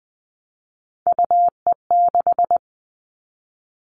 Morse code, heard as a keyed tone, about 20 wpm, 700 Hz. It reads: UE6
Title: All Cut Numbers - 3-Digit 20wpm